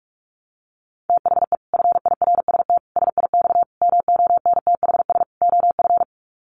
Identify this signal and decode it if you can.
Morse code, heard as a keyed tone, about 44 wpm, 700 Hz. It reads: THE FIRST SIX MONTHS OF
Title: Sets of 5 Words Encore2 45wpm